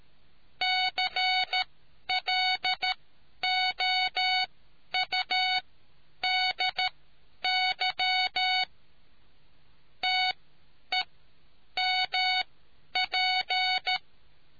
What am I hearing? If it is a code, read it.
Morse code, heard as a keyed tone, about 13 wpm, spaced as for 9 wpm, 740 Hz. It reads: CLOUDY TEMP